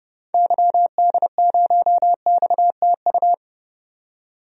Morse code, heard as a keyed tone, about 30 wpm, 700 Hz. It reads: YD0XTU
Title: DXCC-100 Call Signs - 6 Characters - 30wpm